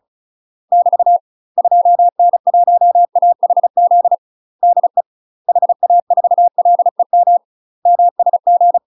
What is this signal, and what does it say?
Morse code, heard as a keyed tone, about 34 wpm, 700 Hz. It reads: <BT> 2N1AHZ DE HA4LEM MSG
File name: UK MASTER QSOs TRAINING / 35WPM / UK MASTER QSOs 0005